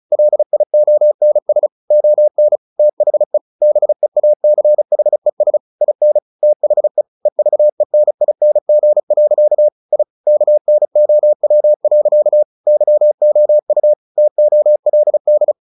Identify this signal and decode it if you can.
Morse code, heard as a keyed tone, about 34 wpm, 600 Hz. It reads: LIONS ON THE BEACHES IN THE EVENING. I KNOW. YOU TOLD
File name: the old man and the sea / the old man and the sea 1 35wpm